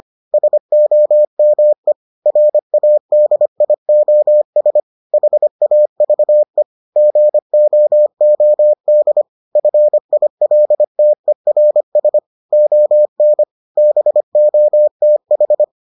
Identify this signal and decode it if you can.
Morse code, heard as a keyed tone, about 25 wpm, 600 Hz. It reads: SOME RADIOS HAVE GOOD FILTERS ON BOTH